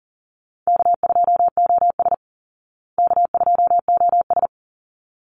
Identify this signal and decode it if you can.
Morse code, heard as a keyed tone, about 40 wpm, 700 Hz. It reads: K2OS K2OS
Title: Call Signs Repeated 3x - 40wpm